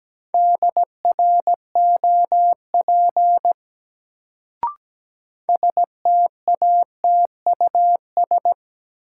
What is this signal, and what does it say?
Morse code, heard as a keyed tone, about 17 wpm, 700 Hz. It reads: DROP STATUS
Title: Sentences from Top 1400 Words - New Word Drill - 17wpm